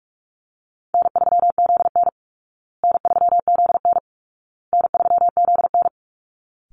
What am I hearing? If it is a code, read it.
Morse code, heard as a keyed tone, about 44 wpm, 700 Hz. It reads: N3ZN N3ZN N3ZN
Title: Call Signs Repeated 2x - 45wpm